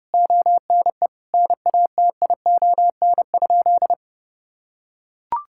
Morse code, heard as a keyed tone, about 32 wpm, 700 Hz. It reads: ONE NATION?
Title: Sentences from Top 1000 Words - Review Part 6 - 30wpm